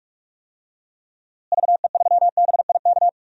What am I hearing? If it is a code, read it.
Morse code, heard as a keyed tone, about 46 wpm, 700 Hz. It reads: VE3BIK
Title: DXCC-100 Call Signs - 6 Characters - Repeated 3x - 45wpm